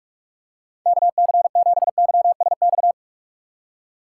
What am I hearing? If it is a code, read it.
Morse code, heard as a keyed tone, about 42 wpm, 700 Hz. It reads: KC7YSX
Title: USA Call Signs - 4x Character Spacing - 45wpm